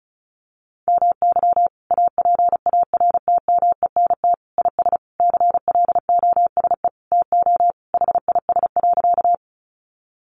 Morse code, heard as a keyed tone, about 34 wpm, 700 Hz. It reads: MY APARTMENT IS CLOSE TO HIS.
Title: Sentences from Top 900 Words 35wpm